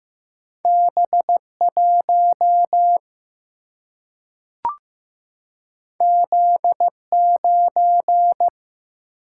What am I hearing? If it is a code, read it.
Morse code, heard as a keyed tone, about 15 wpm, 700 Hz. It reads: B 1 Z 9
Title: Single-Letter Single-Number 15wpm